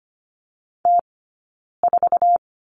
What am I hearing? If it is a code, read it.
Morse code, heard as a keyed tone, about 25 wpm, spaced as for 9 wpm, 690 Hz. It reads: T4
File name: course / 25:10wpm / 092 Lesson 16d - Letter c - Two Characters 25wpm